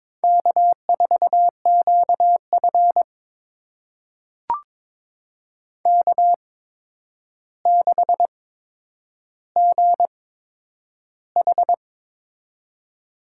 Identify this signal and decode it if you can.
Morse code, heard as a keyed tone, about 22 wpm, spaced as for 9 wpm, 700 Hz. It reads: K4QF K6GH
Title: USA Call Signs - 4 Characters - 8x Character Spacing - 22wpm